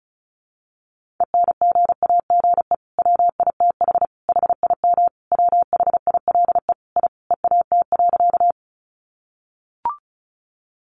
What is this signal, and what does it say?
Morse code, heard as a keyed tone, about 34 wpm, 700 Hz. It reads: ENGAGE WITH HIM WHILE I EAT.
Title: Sentences from Top 1500 Words 35wpm